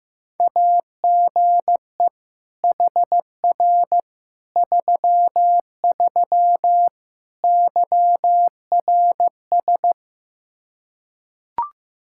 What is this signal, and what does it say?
Morse code, heard as a keyed tone, about 15 wpm, 700 Hz. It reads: AGE HR 33 YRS
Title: K1USN SST QSO Practice 15wpm